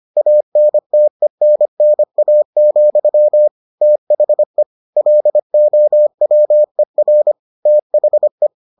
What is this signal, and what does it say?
Morse code, heard as a keyed tone, about 25 wpm, 600 Hz. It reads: ANTENNA, THE LOWER THE